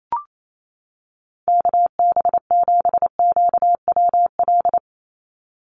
Morse code, heard as a keyed tone, about 28 wpm, 700 Hz. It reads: KB7QWL
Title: USA Call Signs 28wpm